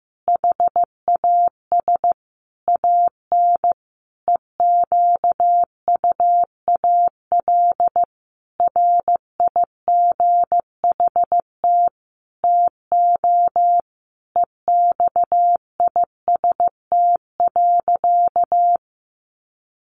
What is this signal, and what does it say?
Morse code, heard as a keyed tone, about 15 wpm, 700 Hz. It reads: HAS AN EQUAL RIGHT TO EXIST.
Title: Sentences from Top 1400 Words 15wpm